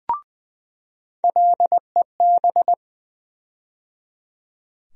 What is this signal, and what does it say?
Morse code, heard as a keyed tone, about 20 wpm, 700 Hz. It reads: LEB
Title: USA Call Sign Postfixes - 3 Letters 20wpm